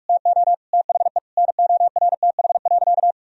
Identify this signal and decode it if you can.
Morse code, heard as a keyed tone, about 46 wpm, 700 Hz. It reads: TO THE NORTH.
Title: Sentences from Top 1000 Words - Review Part 1 - 45wpm